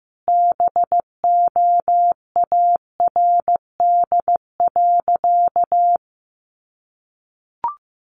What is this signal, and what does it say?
Morse code, heard as a keyed tone, about 15 wpm, 700 Hz. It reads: BOARD.
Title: Sentences from Top 1500 Words 15wpm